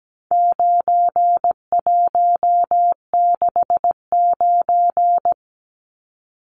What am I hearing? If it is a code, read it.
Morse code, heard as a keyed tone, about 17 wpm, 700 Hz. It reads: 9169
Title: Numbers - 4-Digit 17wpm